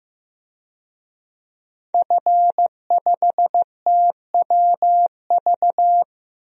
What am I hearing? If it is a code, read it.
Morse code, heard as a keyed tone, about 15 wpm, 700 Hz. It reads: F5TWV
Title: DXCC-100 Call Signs - 5 Characters - Repeated 3x - 15wpm